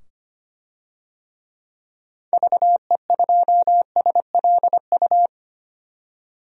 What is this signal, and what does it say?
Morse code, heard as a keyed tone, about 25 wpm, 700 Hz. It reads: VE2SLU
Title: DXCC-100 Call Signs - 6 Characters - 25wpm